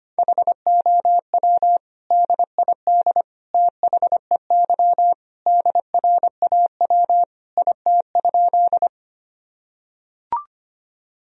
Morse code, heard as a keyed tone, about 25 wpm, 700 Hz. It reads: HOW DID THEY DRAW IT?
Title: Sentences from Top 1000 Words - Review Part 1 - 25wpm